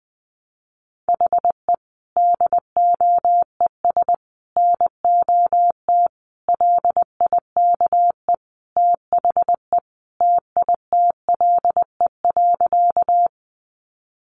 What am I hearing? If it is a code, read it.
Morse code, heard as a keyed tone, about 20 wpm, 700 Hz. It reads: HE DOES NOT LIKE THE TITLE.
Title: Sentences from Top 1200 Words 20wpm